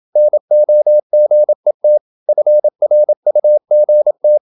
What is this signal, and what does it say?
Morse code, heard as a keyed tone, about 26 wpm, 600 Hz. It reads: NOGET FRUGT
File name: Ferie QRQ 27wpm